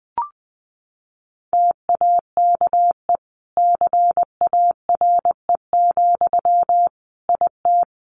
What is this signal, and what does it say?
Morse code, heard as a keyed tone, about 20 wpm, 700 Hz. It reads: TAKE CARE, IT
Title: Sentences from Top 1100 Words 20wpm